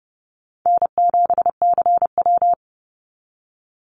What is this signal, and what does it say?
Morse code, heard as a keyed tone, about 30 wpm, 700 Hz. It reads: N7CW
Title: NA SOTA Activator Callsigns 30wpm